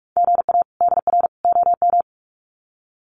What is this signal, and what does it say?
Morse code, heard as a keyed tone, about 44 wpm, 700 Hz. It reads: GA DR OM
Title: Common QSO Phrases 45wpm